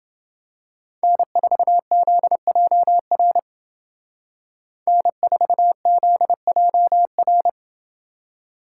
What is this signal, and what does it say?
Morse code, heard as a keyed tone, about 28 wpm, 700 Hz. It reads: N4ZJR N4ZJR